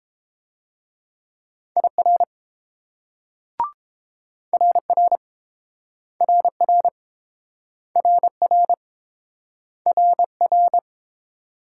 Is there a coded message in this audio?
Yes: Morse code from 1.8 to 10.8 s, heard as a keyed tone, about 30 wpm, 700 Hz. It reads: IR RR RR RR RR